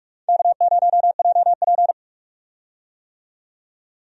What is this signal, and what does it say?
Morse code, heard as a keyed tone, about 46 wpm, 700 Hz. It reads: K0JP
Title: USA Call Signs - 4 Characters - 45wpm